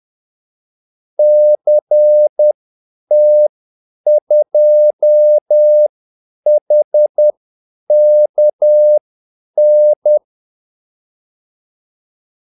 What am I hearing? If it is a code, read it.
Morse code, heard as a keyed tone, about 10 wpm, spaced as for 8 wpm, 600 Hz. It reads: CT2HKN
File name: Callsigns-10-8wpm